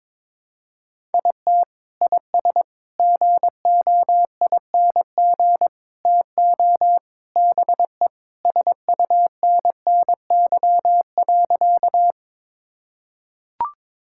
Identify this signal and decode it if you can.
Morse code, heard as a keyed tone, about 22 wpm, 700 Hz. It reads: IT IS GOING TO BE SUNNY.
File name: practice / 22wpm / Sentences with 4-Word Phrases - 22wpm